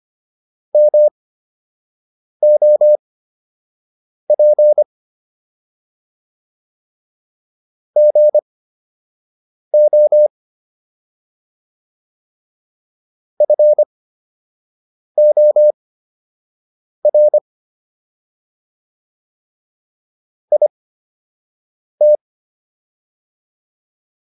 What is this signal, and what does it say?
Morse code, heard as a keyed tone, about 25 wpm, spaced as for 6 wpm, 600 Hz. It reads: MOP GO FOR IT